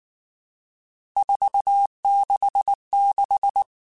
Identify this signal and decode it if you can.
Morse code, heard as a keyed tone, about 19 wpm, 800 Hz. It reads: 466
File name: RSTs / RST-mp3s / Various RST @18